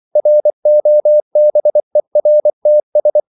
Morse code, heard as a keyed tone, about 24 wpm, 600 Hz. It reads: ROBERTS
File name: CW / Practice / EnglishQuotations0424